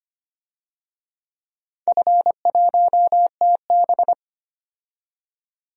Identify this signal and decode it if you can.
Morse code, heard as a keyed tone, about 25 wpm, 700 Hz. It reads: F1TB
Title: DXCC-100 Call Signs - 4 Characters - 25wpm